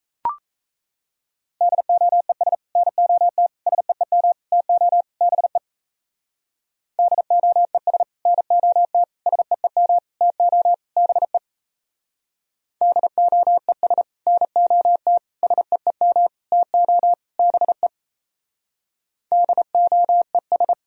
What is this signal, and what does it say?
Morse code, heard as a keyed tone, about 35 wpm, 700 Hz. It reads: DOES NOT SEEM TO BE DOES NOT SEEM TO BE DOES NOT SEEM TO BE DOES